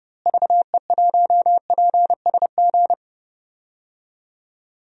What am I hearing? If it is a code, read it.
Morse code, heard as a keyed tone, about 30 wpm, 700 Hz. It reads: VE1PSG
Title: DXCC-100 Call Signs - 6 Characters - Repeated 3x - 30wpm